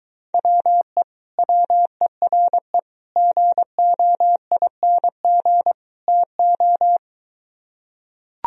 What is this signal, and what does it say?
Morse code, heard as a keyed tone, about 23 wpm, 700 Hz. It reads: WE WERE GOING TO